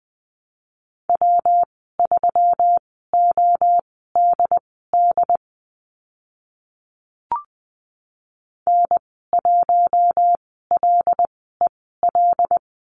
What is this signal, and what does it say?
Morse code, heard as a keyed tone, about 20 wpm, spaced as for 15 wpm, 700 Hz. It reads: W3ODD N1LEL
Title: 140 Lesson 23g - Letter f - Call Signs 20wpm